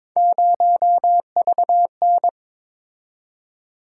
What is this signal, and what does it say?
Morse code, heard as a keyed tone, about 21 wpm, 700 Hz. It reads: 0VN
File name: practice / 22wpm / Three Letter-Number Combinations - 22wpm